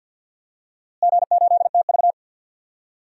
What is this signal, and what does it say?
Morse code, heard as a keyed tone, about 52 wpm, 700 Hz. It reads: G8TV
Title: DXCC-100 Call Signs - 4 Characters - Repeated 2x - 50wpm